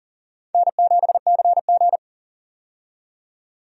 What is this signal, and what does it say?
Morse code, heard as a keyed tone, about 39 wpm, 700 Hz. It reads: N7CG
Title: USA Call Signs - 4 Characters - 40wpm